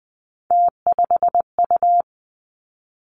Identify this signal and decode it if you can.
Morse code, heard as a keyed tone, about 20 wpm, 700 Hz. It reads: T5U
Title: Three Challenging Letter-Number Combinations 20wpm